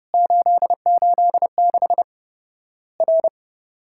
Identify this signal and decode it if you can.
Morse code, heard as a keyed tone, about 30 wpm, 700 Hz. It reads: 886 R
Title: Worked All Europe DX Contest - QTC Exchange - 30wpm